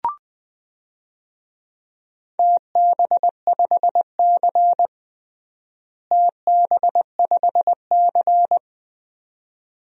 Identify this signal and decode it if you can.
Morse code, heard as a keyed tone, about 21 wpm, 700 Hz. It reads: TB5C TB5C
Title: DXCC-100 Call Signs - 4 Characters - Repeated 2x - 20wpm